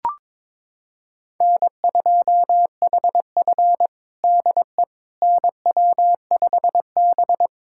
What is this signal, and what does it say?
Morse code, heard as a keyed tone, about 22 wpm, 700 Hz. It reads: N2HF DE NW5B
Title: USA Callsign DE Callsign - 4 Characters - 22wpm